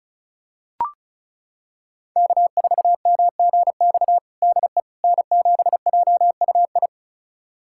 Morse code, heard as a keyed tone, about 36 wpm, 700 Hz. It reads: K4MGX DE N7JUI
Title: USA Callsign DE Callsign - 5 Characters - 35wpm